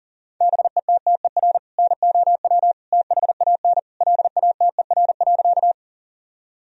Morse code, heard as a keyed tone, about 40 wpm, 700 Hz. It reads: BETTER NOW THAN LATER.
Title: Sentences from Top 1000 Words - Review Part 4 - 40wpm